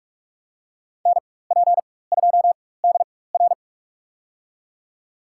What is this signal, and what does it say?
Morse code, heard as a keyed tone, about 46 wpm, spaced as for 21 wpm, 700 Hz. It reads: NP2DR
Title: USA Call Signs - 4x Character Spacing - 45wpm